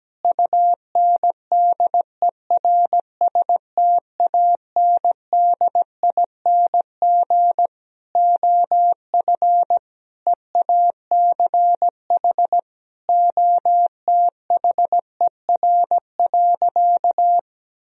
Morse code, heard as a keyed tone, about 17 wpm, 700 Hz. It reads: UNDERSTANDING OF EACH OTHER.
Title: Sentences from Top 1000 Words - Review Part 3 - 17wpm